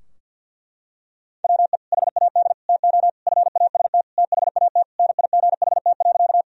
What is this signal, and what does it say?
Morse code, heard as a keyed tone, about 53 wpm, 700 Hz. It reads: WE HAD TO FAST THAT NIGHT.
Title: Sentences from Top 1000 Words - Review Part 2 - 50wpm